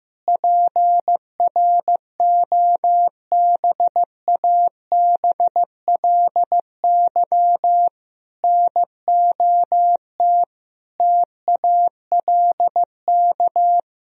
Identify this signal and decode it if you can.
Morse code, heard as a keyed tone, about 15 wpm, 700 Hz. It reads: PROBABLY NOT TALK
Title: Sentences from Top 1000 Words - Review Part 4 - 15wpm